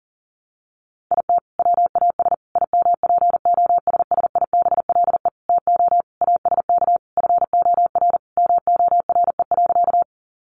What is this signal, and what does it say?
Morse code, heard as a keyed tone, about 40 wpm, 700 Hz. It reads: IT WAS IMPOSSIBLE TO ASK FOR MORE.
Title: Sentences from Top 1100 Words 40wpm